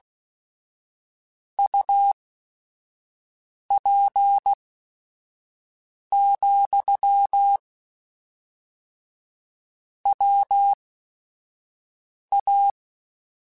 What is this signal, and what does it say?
Morse code, heard as a keyed tone, about 16 wpm, spaced as for 5 wpm, 800 Hz. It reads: UP, WA
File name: QSOs / QSOs-mp3s / kc7sac@5